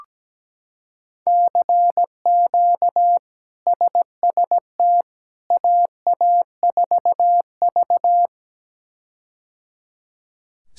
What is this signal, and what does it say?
Morse code, heard as a keyed tone, about 17 wpm, 700 Hz. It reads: CQ SST AA4V
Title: K1USN SST QSO Practice - 17wpm